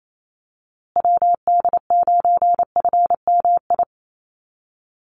Morse code, heard as a keyed tone, about 28 wpm, 700 Hz. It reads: WD9FMI
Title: USA Call Signs Repeated 3x 28wpm